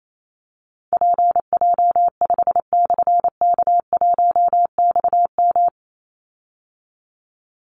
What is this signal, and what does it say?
Morse code, heard as a keyed tone, about 28 wpm, 700 Hz. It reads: PJ5/K1XM
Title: Call Signs Repeated 3x - 28wpm